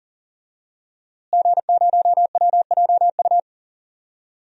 Morse code, heard as a keyed tone, about 37 wpm, 700 Hz. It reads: G0WJU
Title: DXCC-100 Call Signs - 5 Characters - Repeated 3x - 40wpm